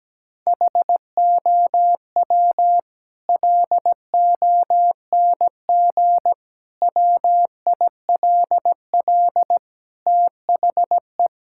Morse code, heard as a keyed tone, about 17 wpm, 700 Hz. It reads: HOW LONG WILL THE
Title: Sentences from Top 1000 Words - Review Part 5 - 17wpm